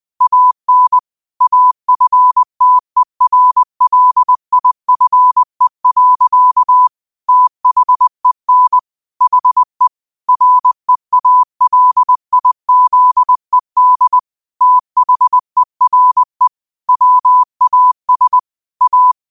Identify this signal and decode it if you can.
Morse code, heard as a keyed tone, about 20 wpm, 1000 Hz. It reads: AN AFTERLIFE. THEN HE REALIZED THERE WAS A
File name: qotdfiles / 20wpm